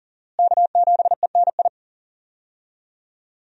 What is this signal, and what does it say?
Morse code, heard as a keyed tone, about 40 wpm, 700 Hz. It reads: K7ENI
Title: USA Call Signs - 5 Characters - Repeated 2x - 40wpm